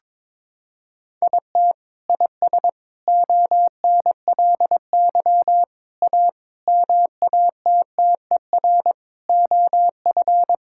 Morse code, heard as a keyed tone, about 22 wpm, 700 Hz. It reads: IT IS ONLY A MATTER OF
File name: Sentences with 9-Word Phrases - 22wpm